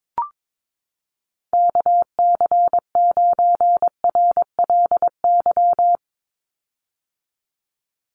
Morse code, heard as a keyed tone, about 22 wpm, 700 Hz. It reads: KC9RLY
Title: USA Call Signs 22wpm